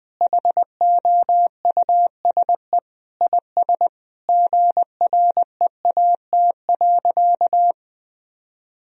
Morse code, heard as a keyed tone, about 20 wpm, 700 Hz. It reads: HOUSE IS GREAT.
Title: Sentences from Top 1000 Words - Review Part 1 - 20wpm